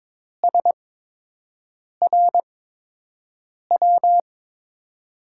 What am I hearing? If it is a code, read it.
Morse code, heard as a keyed tone, about 22 wpm, spaced as for 6 wpm, 700 Hz. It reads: SRW
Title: USA Call Signs - 5 Characters - 8x Character Spacing - 22wpm